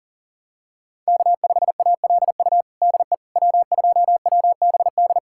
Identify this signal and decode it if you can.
Morse code, heard as a keyed tone, about 40 wpm, 700 Hz. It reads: K5ALU DE W2WBD
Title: USA Callsign DE Callsign - 5 Characters - 40wpm